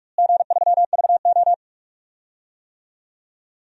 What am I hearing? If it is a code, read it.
Morse code, heard as a keyed tone, about 45 wpm, 700 Hz. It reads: G3VO
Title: DXCC-100 Call Signs - 4 Characters - Repeated 3x - 45wpm